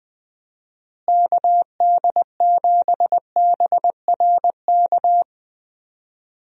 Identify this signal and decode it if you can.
Morse code, heard as a keyed tone, about 21 wpm, 700 Hz. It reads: KD7BRK
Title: USA Call Signs - 6 Characters - Repeated 3x - 20wpm